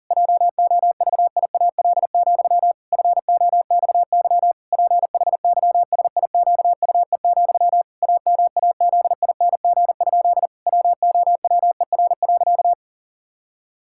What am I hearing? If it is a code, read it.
Morse code, heard as a keyed tone, about 40 wpm, 700 Hz. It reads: JOVIAL, FOXY PHYSIQUE, AMAZING? WOWER.
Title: Pangram Challenge v2 - 40wpm